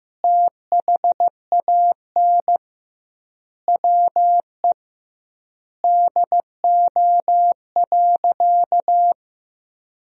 Morse code, heard as a keyed tone, about 16 wpm, 700 Hz. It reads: THAN WE DO.
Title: 2x Word-Spacing - Sentences from Top 100 Words - 15wpm